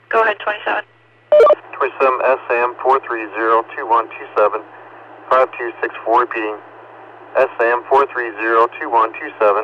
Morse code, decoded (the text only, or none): none